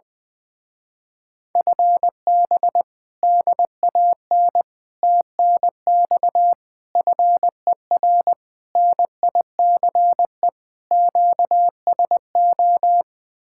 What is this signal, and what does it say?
Morse code, heard as a keyed tone, about 20 wpm, 700 Hz. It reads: FB DAN TNX FER NICE QSO
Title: Speed-Racing Common QSO Phrases - 20wpm